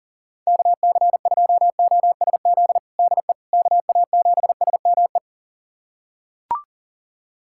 Morse code, heard as a keyed tone, about 37 wpm, 700 Hz. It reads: KC2OSZ DE KA7SME
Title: USA Callsign DE Callsign - 6 Characters - 40wpm